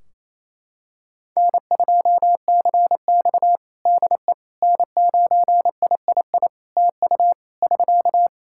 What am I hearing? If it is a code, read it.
Morse code, heard as a keyed tone, about 28 wpm, 700 Hz. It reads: N2CX DE N9III TU <SK>
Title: K1USN SST QSO Practice 28wpm